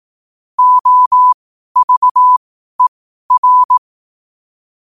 Morse code, heard as a keyed tone, about 18 wpm, spaced as for 12 wpm, 1000 Hz. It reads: OVER
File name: qotdfiles / 12wpm